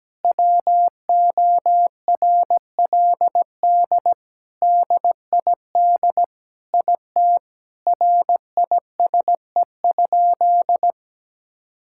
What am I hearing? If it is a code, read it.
Morse code, heard as a keyed tone, about 17 wpm, 700 Hz. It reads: WORLD DID IT RISE?
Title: Sentences from Top 1000 Words - Review Part 6 - 17wpm